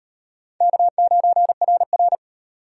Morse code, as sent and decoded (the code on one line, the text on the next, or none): -.- ----. .-. .-.
K9RR